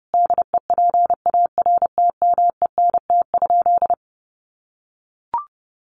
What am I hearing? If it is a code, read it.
Morse code, heard as a keyed tone, about 30 wpm, 700 Hz. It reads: DEPARTMENT?
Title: Sentences from Top 1200 Words 30wpm